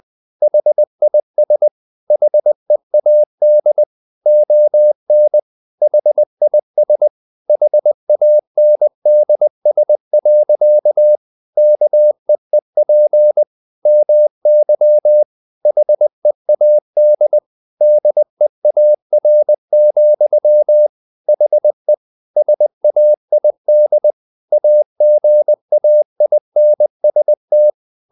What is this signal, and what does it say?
Morse code, heard as a keyed tone, about 20 wpm, 600 Hz. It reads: HIS HEAD ON HIS HANDS. KEEP MY HEAD DEAR, HE SAID AGAINST
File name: the old man and the sea part 2 20wpm